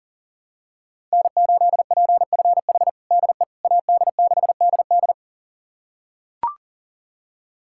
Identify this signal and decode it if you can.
Morse code, heard as a keyed tone, about 37 wpm, 700 Hz. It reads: N8PFH DE AD6DD
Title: USA Callsign DE Callsign - 5 Characters - 40wpm